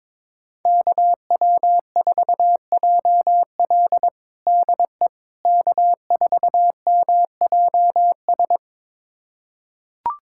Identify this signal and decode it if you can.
Morse code, heard as a keyed tone, about 22 wpm, 700 Hz. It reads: KW4JL DE K4MJS
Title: USA Callsign DE Callsign - 5 Characters - 22wpm